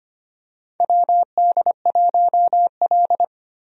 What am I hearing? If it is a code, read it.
Morse code, heard as a keyed tone, about 25 wpm, 700 Hz. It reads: WD1L
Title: USA Call Signs - 4 Characters - 25wpm